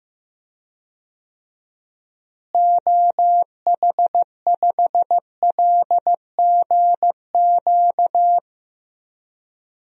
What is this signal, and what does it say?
Morse code, heard as a keyed tone, about 16 wpm, 700 Hz. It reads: OH5LGQ
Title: DXCC-100 Call Signs - 6 Characters - Repeated 2x - 15wpm